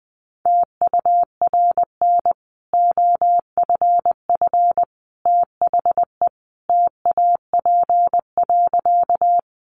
Morse code, heard as a keyed tone, about 20 wpm, 700 Hz. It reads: TURN OFF THE TAP.
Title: Sentences from Top 1400 Words 20wpm